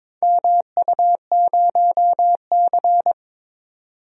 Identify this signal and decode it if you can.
Morse code, heard as a keyed tone, about 21 wpm, 700 Hz. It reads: MU0C
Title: DXCC-100 Call Signs - 4 Characters - 22wpm